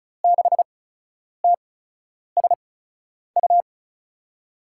Morse code, heard as a keyed tone, about 36 wpm, spaced as for 10 wpm, 700 Hz. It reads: 6TSU
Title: USA Call Signs - 8x Character Spacing - 35wpm